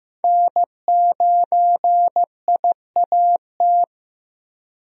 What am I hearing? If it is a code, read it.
Morse code, heard as a keyed tone, about 15 wpm, 700 Hz. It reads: N9IAT
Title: USA Call Signs - 5 Characters - Repeated 2x - 15wpm